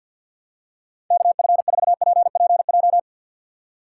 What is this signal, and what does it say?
Morse code, heard as a keyed tone, about 52 wpm, 700 Hz. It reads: KF4PPJ
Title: USA Call Signs - 4x Character Spacing - 50wpm